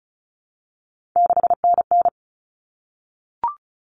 Morse code, heard as a keyed tone, about 34 wpm, 700 Hz. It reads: 6NN
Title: Common Cut Numbers - 3-Digit 35wpm